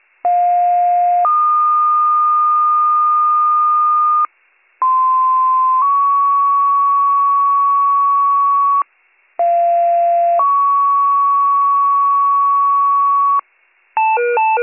Two-tone page, heard as a one-second tone, then a three-second tone, about 710 then 1220 Hz, about 1040 then 1120 Hz, about 690 then 1120 Hz. It is heard from 0.2 to 13.4 s.